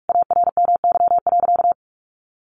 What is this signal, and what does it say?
Morse code, heard as a keyed tone, about 44 wpm, 700 Hz. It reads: ARMY.
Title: Sentences from Top 1200 Words 45wpm